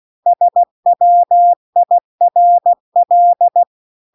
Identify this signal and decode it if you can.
Morse code, heard as a keyed tone, about 16 wpm, 700 Hz. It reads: SWIRL
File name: CW-The-Big-Twoheaded-River-2-16WPM